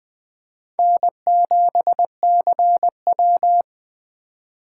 Morse code, heard as a keyed tone, about 21 wpm, 700 Hz. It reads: N7CW
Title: 2023 SOTA Chaser Call Signs - 20wpm